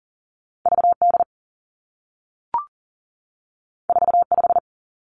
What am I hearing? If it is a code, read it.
Morse code, heard as a keyed tone, about 40 wpm, 700 Hz. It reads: V D 4 5
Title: Two Challenging Letter-Number Combinations 40wpm